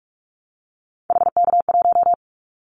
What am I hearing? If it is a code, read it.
Morse code, heard as a keyed tone, about 44 wpm, 700 Hz. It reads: HK1
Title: Callsign Prefixes for Top 100 DXCC - 45wpm